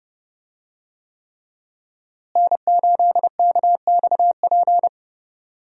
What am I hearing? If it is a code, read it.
Morse code, heard as a keyed tone, about 30 wpm, 700 Hz. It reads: N8KXP
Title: USA Call Signs - 5 Characters - Repeated 3x - 30wpm